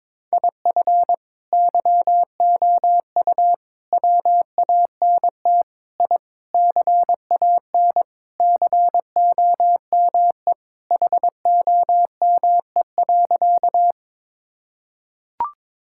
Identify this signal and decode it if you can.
Morse code, heard as a keyed tone, about 22 wpm, 700 Hz. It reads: IF YOU WANT I CAN COME HOME.
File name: practice / 22wpm / Sentences with 3-Word Phrases - 22wpm